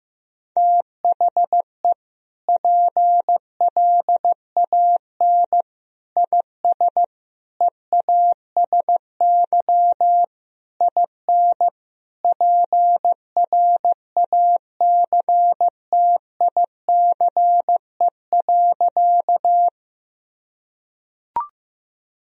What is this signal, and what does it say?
Morse code, heard as a keyed tone, about 15 wpm, 700 Hz. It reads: THE PLAN IS EASY IN PRACTICE.
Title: Sentences from Top 1000 Words - Review Part 4 - 15wpm